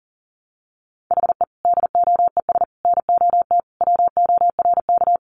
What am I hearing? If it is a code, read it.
Morse code, heard as a keyed tone, about 40 wpm, 700 Hz. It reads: HE DOES NOT WORK
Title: Additional Easy Phrases 40wpm